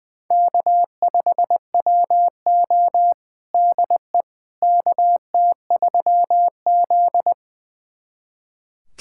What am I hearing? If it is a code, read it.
Morse code, heard as a keyed tone, about 21 wpm, 700 Hz. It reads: K5WO DE KT3Z
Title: USA Callsign DE Callsign - 4 Characters - 20wpm